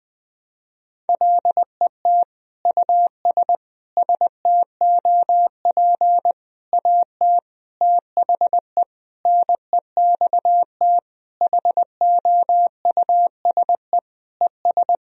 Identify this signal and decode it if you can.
Morse code, heard as a keyed tone, about 20 wpm, 700 Hz. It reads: LET US STOP AT THE NEXT HOUSE ES
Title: Sentences from Top 400 Words and Abbreviations 20wpm